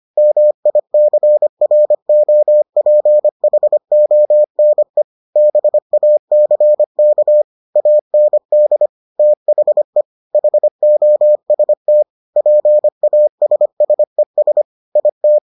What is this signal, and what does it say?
Morse code, heard as a keyed tone, about 25 wpm, 600 Hz. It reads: MICROPHONE BACK AND THE HOST PASSES IT